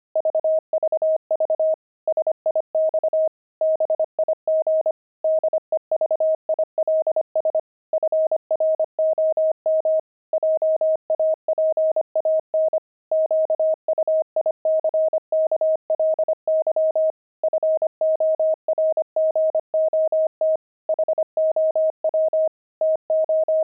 Morse code, heard as a keyed tone, about 25 wpm, 630 Hz. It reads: VVV SIX BIG DEVILS FROM JAPAN QUICKLY FORGOT HOW TO
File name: pangram-fs-25-cs-25-devils